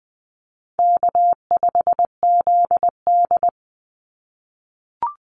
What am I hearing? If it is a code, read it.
Morse code, heard as a keyed tone, about 20 wpm, 700 Hz. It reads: K5ZD
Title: Easy Call Signs - 20wpm